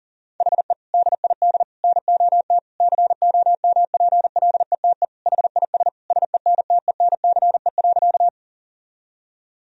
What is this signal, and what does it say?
Morse code, heard as a keyed tone, about 40 wpm, 700 Hz. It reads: HE DID NOT COMPLETE HIS SENTENCE.
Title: Sentences from Top 1000 Words - Review Part 1 - 40wpm